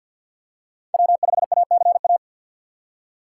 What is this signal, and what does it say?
Morse code, heard as a keyed tone, about 50 wpm, 700 Hz. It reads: W5ACA
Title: USA Call Signs - 5 Characters - 4x Character Spacing - 50wpm